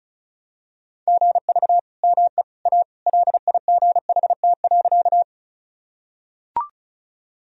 Morse code, heard as a keyed tone, about 36 wpm, 700 Hz. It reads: GV ME A LIGHT.
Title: Sentences from Top 200 Words and Abbreviations 35wpm